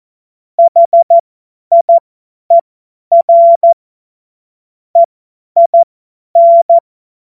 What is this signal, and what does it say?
Morse code, heard as a keyed tone, about 14 wpm, spaced as for 10 wpm, 680 Hz. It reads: HIER EIN
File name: Morsekurs / Witze / Witz 02. Dezember 2025 50 F70